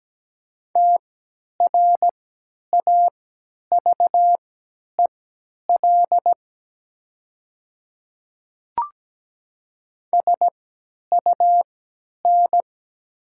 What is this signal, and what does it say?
Morse code, heard as a keyed tone, about 17 wpm, spaced as for 10 wpm, 700 Hz. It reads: TRAVEL SUN
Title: 3x Character Spacing - Top 500 Words - 17wpm